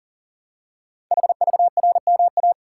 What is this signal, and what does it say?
Morse code, heard as a keyed tone, about 39 wpm, 700 Hz. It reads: HVRMA
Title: Code Groups - 5 Letters - 40wpm